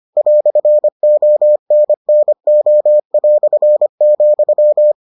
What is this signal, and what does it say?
Morse code, heard as a keyed tone, about 25 wpm, 600 Hz. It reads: "ONNO",